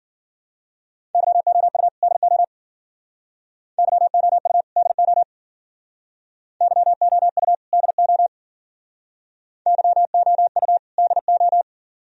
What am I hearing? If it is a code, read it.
Morse code, heard as a keyed tone, about 49 wpm, 700 Hz. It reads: YOU DO YOU DO YOU DO YOU DO
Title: Speed-Racing - Sets of 2 Words 40wpm